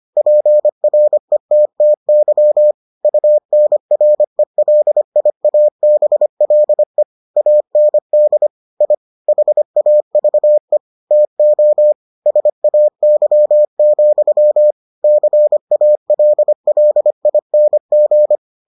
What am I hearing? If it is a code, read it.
Morse code, heard as a keyed tone, about 25 wpm, 600 Hz. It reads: PRETTY UNRELIABLE AND I HAVE TO SAY, CALLING